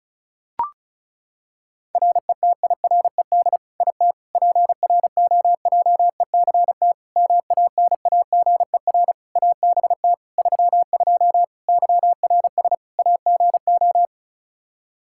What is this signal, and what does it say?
Morse code, heard as a keyed tone, about 36 wpm, 700 Hz. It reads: RETIRED IT PROJECT MANAGER ABT 32 YRS AGO
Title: Practice QSO - Exchange 3 - More Info - 35wpm